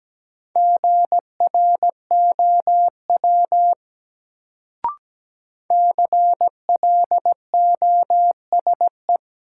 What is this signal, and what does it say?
Morse code, heard as a keyed tone, about 17 wpm, 700 Hz. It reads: GROW CLOSE
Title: Sentences from Top 300 Words - New Word Drill - 17wpm